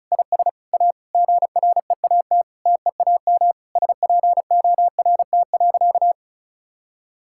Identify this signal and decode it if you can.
Morse code, heard as a keyed tone, about 36 wpm, 700 Hz. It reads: IS A GREAT TEAM SPORT.
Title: Sentences from Top 1600 Words 35wpm